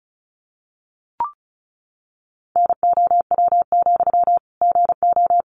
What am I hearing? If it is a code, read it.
Morse code, heard as a keyed tone, about 34 wpm, 700 Hz. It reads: NOW, GO
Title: Sentences from Top 200 Words 35wpm